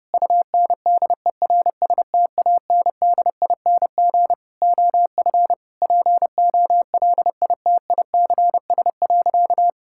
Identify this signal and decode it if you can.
Morse code, heard as a keyed tone, about 32 wpm, 700 Hz. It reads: UNDERSTANDING OF POLITICS.
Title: Sentences from Top 1000 Words - Review Part 6 - 30wpm